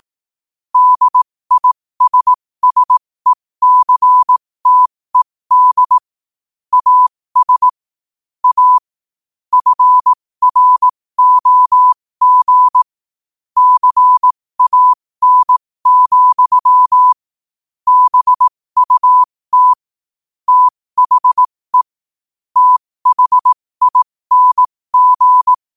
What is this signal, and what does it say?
Morse code, heard as a keyed tone, about 18 wpm, spaced as for 15 wpm, 1000 Hz. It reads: DISSECTED AS A FROG CAN, BUT THE THING
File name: qotdfiles / 15wpm